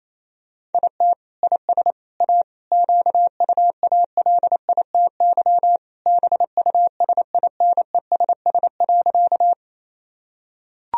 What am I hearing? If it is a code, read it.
Morse code, heard as a keyed tone, about 28 wpm, 700 Hz. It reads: IT IS A QUALITY BUSINESS.
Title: Sentences from Top 1000 Words - Review Part 4 - 28wpm